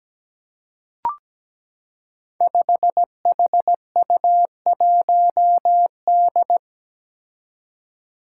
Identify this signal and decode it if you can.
Morse code, heard as a keyed tone, about 17 wpm, 700 Hz. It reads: 5HU1D
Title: Five Challenging Letter-Number Combinations 17wpm